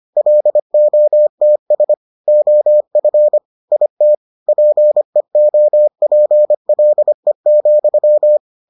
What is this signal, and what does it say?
Morse code, heard as a keyed tone, about 25 wpm, 600 Hz. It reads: LOTS OF IT PEOPLE,